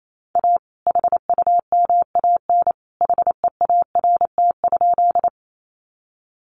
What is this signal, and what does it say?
Morse code, heard as a keyed tone, about 28 wpm, 700 Hz. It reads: A HUMAN HEART?
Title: Sentences from Top 1200 Words 28wpm